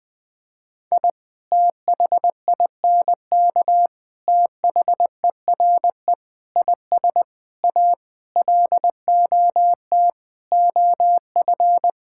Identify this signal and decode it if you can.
Morse code, heard as a keyed tone, about 20 wpm, 700 Hz. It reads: I THINK THERE IS A LOT OF
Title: Sentences with 6-Word Phrases - Spoken Beginning - 20wpm